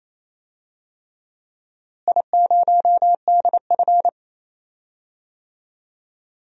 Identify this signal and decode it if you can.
Morse code, heard as a keyed tone, about 28 wpm, 700 Hz. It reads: I0DF
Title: DXCC-100 Call Signs - 4 Characters - Repeated 3x - 28wpm